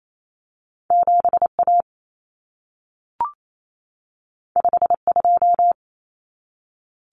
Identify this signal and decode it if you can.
Morse code, heard as a keyed tone, about 28 wpm, 700 Hz. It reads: 7A 52
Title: Common Cut Numbers - 2-Digit 28wpm